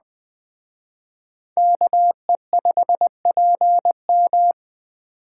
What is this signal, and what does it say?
Morse code, heard as a keyed tone, about 24 wpm, 700 Hz. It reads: KE5PM
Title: Speed Racing - USA 2x2 Call Signs 20wpm